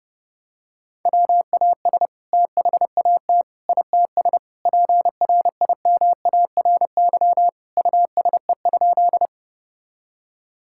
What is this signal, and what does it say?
Morse code, heard as a keyed tone, about 30 wpm, 700 Hz. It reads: WAS THAT ITS PRIMARY USE?
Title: Sentences from Top 1000 Words - Review Part 5 - 30wpm